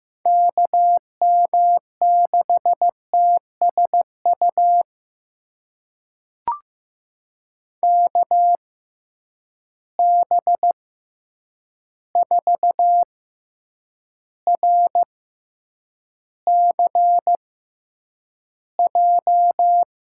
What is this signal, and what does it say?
Morse code, heard as a keyed tone, about 15 wpm, spaced as for 8 wpm, 700 Hz. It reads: KM6TSU KB4RCJ
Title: USA Call Signs - 6 Characters - 6x Character Spacing - 15wpm